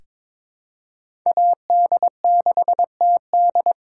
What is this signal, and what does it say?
Morse code, heard as a keyed tone, about 21 wpm, 700 Hz. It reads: AD6TD
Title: USA Callsign DE Callsign - 5 Characters - 22wpm